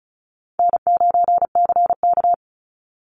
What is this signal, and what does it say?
Morse code, heard as a keyed tone, about 34 wpm, 700 Hz. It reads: N9CK
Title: Easy Call Signs Repeated 2x - 35wpm